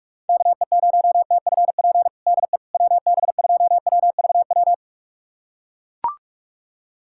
Morse code, heard as a keyed tone, about 42 wpm, 700 Hz. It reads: KE0TFP DE WB2WVW
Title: USA Callsign DE Callsign - 6 Characters - 45wpm